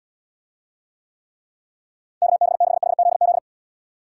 Morse code, heard as a keyed tone, about 78 wpm, 700 Hz. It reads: KB6HZZ